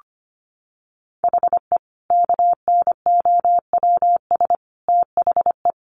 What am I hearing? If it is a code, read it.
Morse code, heard as a keyed tone, about 25 wpm, 700 Hz. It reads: HE KNOWS THE
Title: Additional Easy Phrases 25wpm